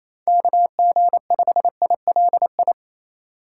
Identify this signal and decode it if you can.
Morse code, heard as a keyed tone, about 35 wpm, 700 Hz. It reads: KG5ILI
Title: Speed Racing - USA 2x3 Call Signs 28wpm